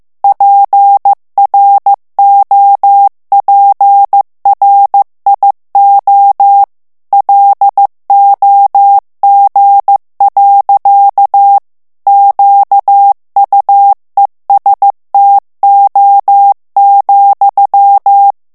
Morse code, heard as a keyed tone, about 15 wpm, 800 Hz. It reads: PROPRIO LOG. QUESTO,
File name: morse mp3 / ITALIEN texte clair 2 vitesse 14 mots minute